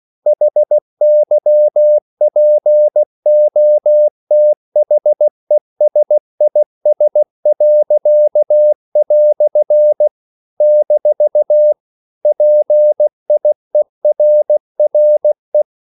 Morse code, heard as a keyed tone, about 16 wpm, 600 Hz. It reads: HYPOTHESIS." - PIERRE
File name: CW / Practice / EnglishQuotations0416